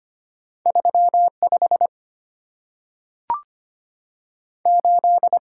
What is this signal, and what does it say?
Morse code, heard as a keyed tone, about 25 wpm, 700 Hz. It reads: 3 5 8
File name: Two Letter-Number Combinations - 25wpm